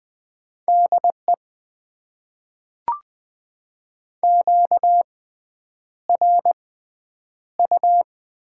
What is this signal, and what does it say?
Morse code, heard as a keyed tone, about 20 wpm, spaced as for 9 wpm, 700 Hz. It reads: DE QRU
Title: Top 90 QSO Elements - 6x Character Spacing - 20wpm